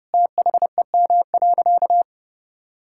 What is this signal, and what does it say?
Morse code, heard as a keyed tone, about 32 wpm, 700 Hz. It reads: THEM.
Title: Sentences from Top 1000 Words - Review Part 5 - 30wpm